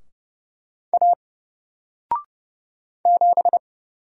Morse code, heard as a keyed tone, about 29 wpm, 700 Hz. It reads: A 7
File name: Single-Letter-Number-30wpm